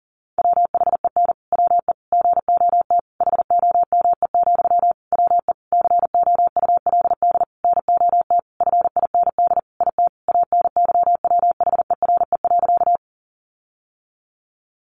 Morse code, heard as a keyed tone, about 40 wpm, 700 Hz. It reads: WHEN WE GOT HOME, WE COULD NOT FIND IT ANYWHERE.
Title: Sentences from Top 1400 Words 40wpm